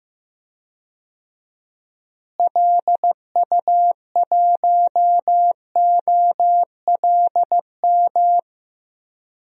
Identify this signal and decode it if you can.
Morse code, heard as a keyed tone, about 15 wpm, 700 Hz. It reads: LU1OLM
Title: DXCC-100 Call Signs - 6 Characters - Repeated 3x - 15wpm